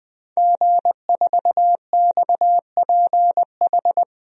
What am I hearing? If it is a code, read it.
Morse code, heard as a keyed tone, about 20 wpm, 700 Hz. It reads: G4XPH
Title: DXCC-100 Call Signs - 5 Characters - 20wpm